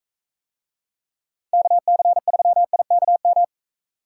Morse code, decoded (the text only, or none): KC3IKM